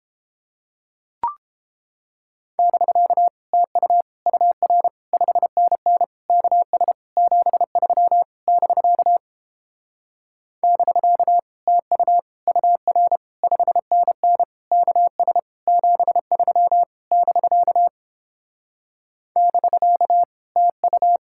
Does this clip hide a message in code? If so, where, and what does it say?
Yes: Morse code from 2.6 to 21.2 s, heard as a keyed tone, about 29 wpm, 700 Hz. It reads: <BK> TU UR 5NN KS 73 <BK> <BK> TU UR 5NN KS 73 <BK> <BK> TU